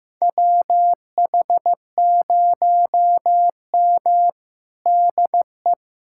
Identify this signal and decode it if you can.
Morse code, heard as a keyed tone, about 16 wpm, 700 Hz. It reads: WH0M DE
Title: USA Callsign DE Callsign - 4 Characters - 15wpm